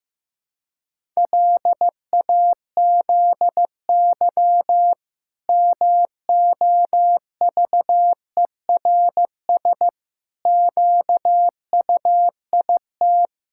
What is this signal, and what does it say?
Morse code, heard as a keyed tone, about 16 wpm, 700 Hz. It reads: LAZY MOVERS QUIT
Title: Pangram Challenge v2 - 15wpm